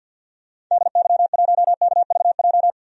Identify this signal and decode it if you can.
Morse code, heard as a keyed tone, about 48 wpm, 700 Hz. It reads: DY1KVJ
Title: DXCC-100 Call Signs - 6 Characters - 50wpm